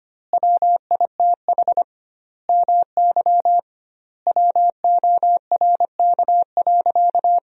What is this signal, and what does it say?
Morse code, heard as a keyed tone, about 25 wpm, 700 Hz. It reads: WITH MY WORK.
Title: 2x Word-Spacing - Sentences from Top 300 Words - 25wpm